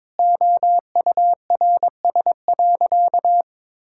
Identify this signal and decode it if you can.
Morse code, heard as a keyed tone, about 22 wpm, 700 Hz. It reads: OURS.
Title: Sentences from Top 1000 Words - Review Part 5 - 22wpm